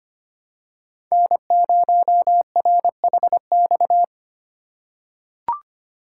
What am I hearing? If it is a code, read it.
Morse code, heard as a keyed tone, about 25 wpm, 700 Hz. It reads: N0RHX
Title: USA Call Signs - 5 Characters - 2x Character Spacing - 25wpm